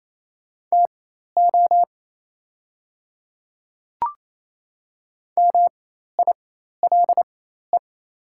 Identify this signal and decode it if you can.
Morse code, heard as a keyed tone, about 28 wpm, spaced as for 13 wpm, 700 Hz. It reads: TO MILE
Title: 4x Character Spacing - Top 300 Words - 28wpm